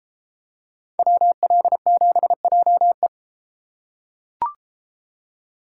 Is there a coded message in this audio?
Yes: Morse code from 1.0 to 3.1 s, heard as a keyed tone, about 31 wpm, 700 Hz. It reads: WL7JE